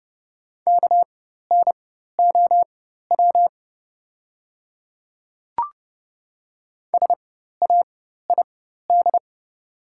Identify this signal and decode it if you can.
Morse code, heard as a keyed tone, about 30 wpm, spaced as for 14 wpm, 700 Hz. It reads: KNOW SAID
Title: 4x Character Spacing - Top 100 Words - 30wpm